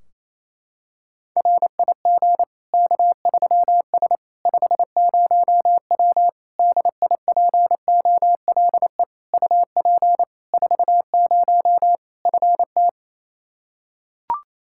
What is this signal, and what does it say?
Morse code, heard as a keyed tone, about 28 wpm, 700 Hz. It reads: RIG K3S 50W DIPOLE UP 40 FT
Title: K1USN SST QSO Practice 28wpm